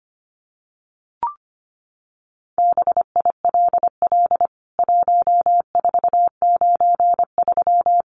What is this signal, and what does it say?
Morse code, heard as a keyed tone, about 25 wpm, 700 Hz. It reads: BILL 1493
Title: CWT Exchange 25wpm